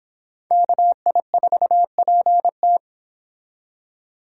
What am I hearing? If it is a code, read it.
Morse code, heard as a keyed tone, about 25 wpm, 700 Hz. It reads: KI4PT